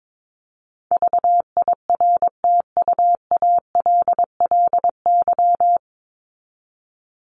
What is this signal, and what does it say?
Morse code, heard as a keyed tone, about 22 wpm, 700 Hz. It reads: VIRTUALLY
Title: Top 1501-1750 Words 22wpm